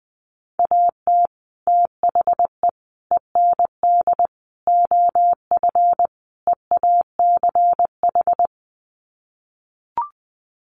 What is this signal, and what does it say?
Morse code, heard as a keyed tone, about 19 wpm, 700 Hz. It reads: AT THE END OF EACH
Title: Sets of 5 Words Encore 20wpm